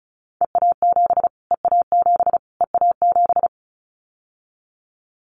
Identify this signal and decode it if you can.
Morse code, heard as a keyed tone, about 34 wpm, 700 Hz. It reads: EA7 EA7 EA7
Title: Callsign Prefixes for Top 25 DXCC - Repeated 3x - 35wpm